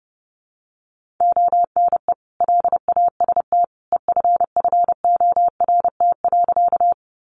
Morse code, heard as a keyed tone, about 30 wpm, 700 Hz. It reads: ONE LAST EFFORT.
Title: Sentences from Top 1200 Words 30wpm